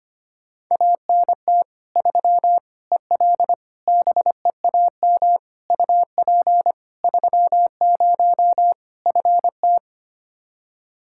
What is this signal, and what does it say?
Morse code, heard as a keyed tone, about 25 wpm, 700 Hz. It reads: ANT 3 EL BEAM UP 30 FT
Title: Practice QSO - Exchange 2 - Info - 25wpm